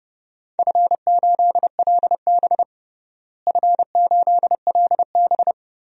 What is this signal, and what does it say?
Morse code, heard as a keyed tone, about 30 wpm, 700 Hz. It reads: F8LB F8LB
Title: DXCC-100 Call Signs - 4 Characters - Repeated 3x - 30wpm